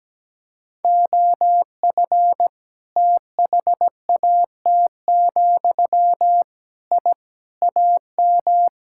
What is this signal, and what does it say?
Morse code, heard as a keyed tone, about 17 wpm, 700 Hz. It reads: OF THAT, I AM
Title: Sentences from Top 1000 Words - Review Part 5 - 17wpm